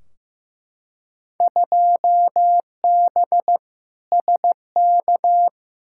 Morse code, heard as a keyed tone, about 15 wpm, 700 Hz. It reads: 2B SK
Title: Field Day Exchanges Encore - 15wpm